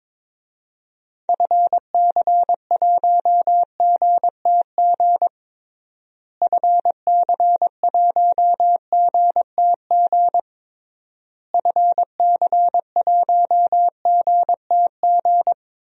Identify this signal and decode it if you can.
Morse code, heard as a keyed tone, about 22 wpm, 700 Hz. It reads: FC1GTG FC1GTG FC1GTG
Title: DXCC-100 Call Signs - 6 Characters - Repeated 3x - 22wpm